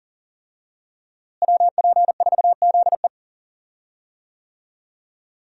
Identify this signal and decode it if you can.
Morse code, heard as a keyed tone, about 40 wpm, 700 Hz. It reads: WP4ZE
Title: USA Call Signs - 5 Characters - Repeated 3x - 40wpm